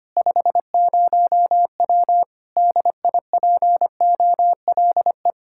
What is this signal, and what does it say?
Morse code, heard as a keyed tone, about 25 wpm, 700 Hz. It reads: 50W DIPOLE
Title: K1USN SST QSO Practice 25wpm